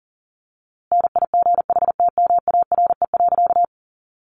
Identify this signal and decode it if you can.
Morse code, heard as a keyed tone, about 40 wpm, 700 Hz. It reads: NIGHTMARE.
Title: A Sound Of Thunder 40wpm